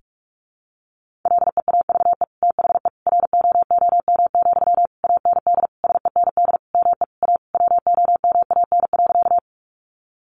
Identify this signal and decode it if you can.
Morse code, heard as a keyed tone, about 44 wpm, 700 Hz. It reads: LEAVE THE ROOM, AND SEND ME A WOMAN.
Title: Sentences from Top 1500 Words 45wpm